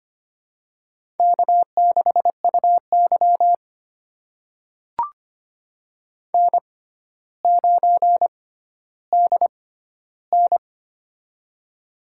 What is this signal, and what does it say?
Morse code, heard as a keyed tone, about 25 wpm, spaced as for 13 wpm, 700 Hz. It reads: K6UY N9DN
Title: USA Call Signs - 4 Characters - 6x Character Spacing - 25wpm